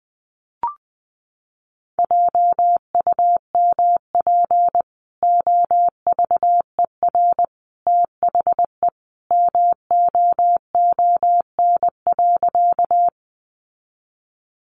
Morse code, heard as a keyed tone, about 20 wpm, 700 Hz. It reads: JUMP OVER THE MOON.
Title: Sentences from Top 1300 Words 20wpm